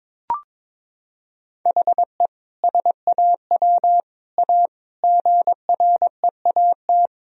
Morse code, heard as a keyed tone, about 22 wpm, 700 Hz. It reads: HE SAW A GREAT
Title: Sentences from Top 1000 Words - Review Part 1 - 22wpm